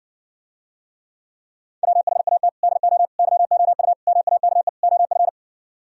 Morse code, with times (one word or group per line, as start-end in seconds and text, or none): 1.8-2.5 WHAT
2.6-3.1 DO
3.2-3.9 YOU
4.1-4.7 MAKE
4.8-5.3 OF